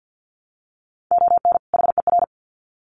Morse code, heard as a keyed tone, about 50 wpm, 700 Hz. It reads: ON HER
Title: Sets of 2 Words 50wpm